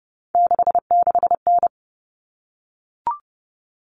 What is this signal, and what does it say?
Morse code, heard as a keyed tone, about 30 wpm, 700 Hz. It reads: 66N
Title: All Cut Numbers - 3-Digit 30wpm